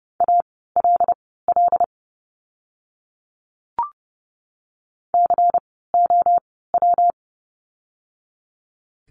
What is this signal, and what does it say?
Morse code, heard as a keyed tone, about 30 wpm, spaced as for 17 wpm, 700 Hz. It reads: ALL COW
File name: course / 30:10wpm / 115 Lesson 19f - Letter w - Words 30wpm